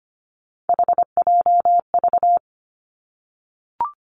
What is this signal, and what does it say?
Morse code, heard as a keyed tone, about 25 wpm, 700 Hz. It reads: HJV
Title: Three Challenging Letter-Number Combinations 25wpm